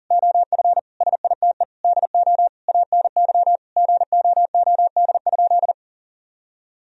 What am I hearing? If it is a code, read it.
Morse code, heard as a keyed tone, about 40 wpm, 700 Hz. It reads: OF SITE DO ANY GOOD?
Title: Sentences from Top 1000 Words - Review Part 4 - 40wpm